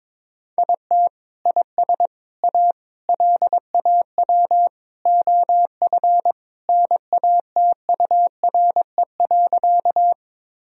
Morse code, heard as a keyed tone, about 22 wpm, 700 Hz. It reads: IT IS A LAW OF NATURE.
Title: Sentences from Top 1000 Words - Review Part 3 - 22wpm